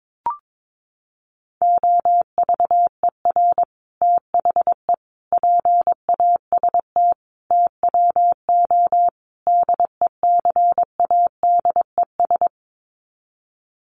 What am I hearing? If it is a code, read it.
Morse code, heard as a keyed tone, about 22 wpm, 700 Hz. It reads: OVER THE PAST TWO DECADES
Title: Sets of 5 Words Encore2 22wpm